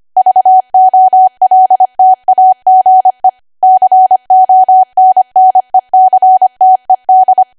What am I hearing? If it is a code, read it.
Morse code, heard as a keyed tone, about 25 wpm, 750 Hz. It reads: VOLTAGE CONNECTED
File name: Archive / 25 WPM / 25WPM